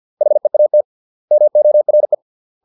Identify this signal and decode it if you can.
Morse code, heard as a keyed tone, about 50 wpm, 600 Hz. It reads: HEAT MORE